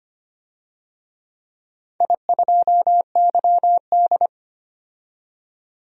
Morse code, heard as a keyed tone, about 25 wpm, 700 Hz. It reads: I2YD
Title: DXCC-100 Call Signs - 4 Characters - 25wpm